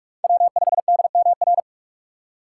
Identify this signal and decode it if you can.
Morse code, heard as a keyed tone, about 46 wpm, 700 Hz. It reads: W5DMR
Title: USA Call Signs - 2x Character Spacing - 45wpm